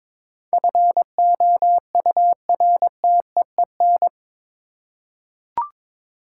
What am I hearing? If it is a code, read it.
Morse code, heard as a keyed tone, about 22 wpm, 700 Hz. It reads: FOURTEEN
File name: practice / 22wpm / Eight Letter Words - 22wpm